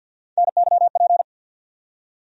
Morse code, heard as a keyed tone, about 51 wpm, 700 Hz. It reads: NYP
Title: USA Call Sign Postfixes - 3 Letters 50wpm